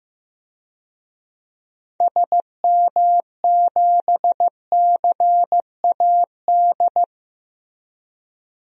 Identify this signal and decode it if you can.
Morse code, heard as a keyed tone, about 15 wpm, 700 Hz. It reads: SM7CAD
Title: DXCC-100 Call Signs - 6 Characters - Repeated 2x - 15wpm